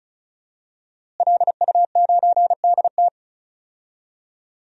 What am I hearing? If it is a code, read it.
Morse code, heard as a keyed tone, about 34 wpm, 700 Hz. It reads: LU9DT
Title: DXCC-100 Call Signs - 5 Characters - Repeated 2x - 35wpm